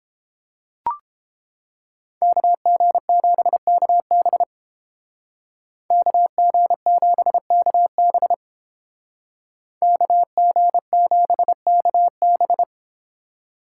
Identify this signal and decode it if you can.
Morse code, heard as a keyed tone, about 30 wpm, 700 Hz. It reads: KG7KB KG7KB KG7KB